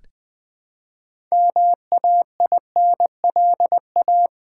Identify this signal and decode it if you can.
Morse code, heard as a keyed tone, about 20 wpm, 700 Hz. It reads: MAINLA
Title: Build A Word - 2 Letters At A Time - 8-Letter Words - 20wpm